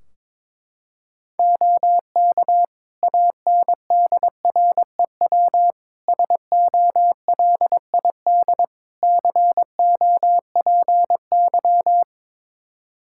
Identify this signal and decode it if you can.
Morse code, heard as a keyed tone, about 22 wpm, 700 Hz. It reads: OK ANDREW SOLID COPY
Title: Practice QSO - Exchange 3 - More Info - 22wpm